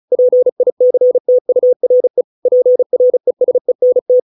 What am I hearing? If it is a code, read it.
Morse code, heard as a keyed tone, about 34 wpm, 500 Hz. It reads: PICTURE PRESENT